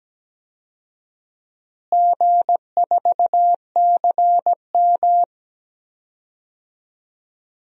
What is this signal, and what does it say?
Morse code, heard as a keyed tone, about 17 wpm, 700 Hz. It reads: G4CM
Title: DXCC-100 Call Signs - 4 Characters - Repeated 3x - 17wpm